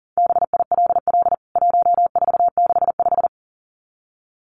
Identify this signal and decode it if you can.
Morse code, heard as a keyed tone, about 40 wpm, 700 Hz. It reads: BILL 1465
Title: CWT Exchange 40wpm